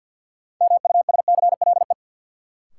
Morse code, heard as a keyed tone, about 51 wpm, 700 Hz. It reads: MUSCLE
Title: Sentences from Top 1300 Words - New Word Drill Alt - 50wpm